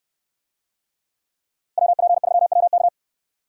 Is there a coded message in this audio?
Yes: Morse code from 1.8 to 2.9 s, heard as a keyed tone, about 69 wpm, 700 Hz. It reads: WP3WK